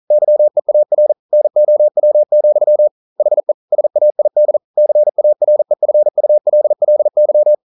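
Morse code, heard as a keyed tone, about 42 wpm, 600 Hz. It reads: YEAR NOW, HE SAID CAREFULLY
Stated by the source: Morse code at 40 wpm, practice text, CW med Gnister.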